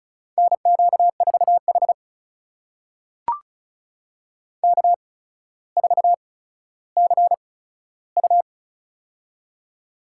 Morse code, heard as a keyed tone, about 36 wpm, spaced as for 15 wpm, 700 Hz. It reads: NQ4H K4CU
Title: USA Call Signs - 4 Characters - 8x Character Spacing - 35wpm